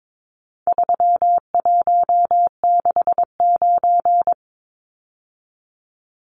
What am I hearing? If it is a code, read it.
Morse code, heard as a keyed tone, about 22 wpm, 700 Hz. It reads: 3169
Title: Numbers - 4-Digit 22wpm